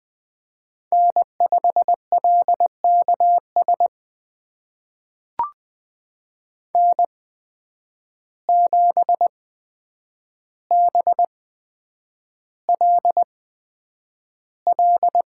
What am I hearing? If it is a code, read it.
Morse code, heard as a keyed tone, about 20 wpm, spaced as for 9 wpm, 700 Hz. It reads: N5LKS N7BLL
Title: USA Call Signs - 5 Characters - 8x Character Spacing - 20wpm